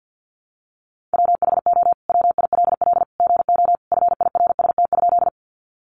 Morse code, heard as a keyed tone, about 50 wpm, 700 Hz. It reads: WHO WILL GO FIRST?
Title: Sentences from Top 100 Words 50wpm